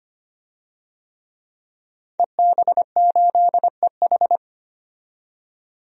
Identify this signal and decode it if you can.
Morse code, heard as a keyed tone, about 25 wpm, 700 Hz. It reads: EB8EH
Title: DXCC-100 Call Signs - 5 Characters - 25wpm